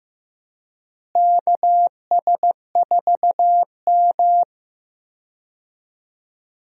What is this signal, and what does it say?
Morse code, heard as a keyed tone, about 15 wpm, 700 Hz. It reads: KS4M
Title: USA Call Signs - 4 Characters - Repeated 3x - 15wpm